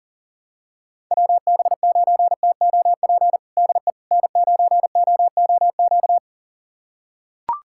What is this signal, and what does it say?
Morse code, heard as a keyed tone, about 40 wpm, 700 Hz. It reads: WB9TOP DE N9OOQ
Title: USA Callsign DE Callsign 40wpm